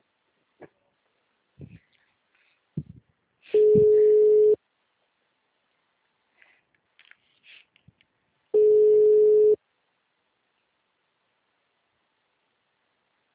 Ringback tone (Europe, 425 Hz).